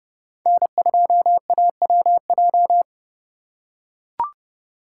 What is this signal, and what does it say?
Morse code, heard as a keyed tone, about 30 wpm, 700 Hz. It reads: N2AWJ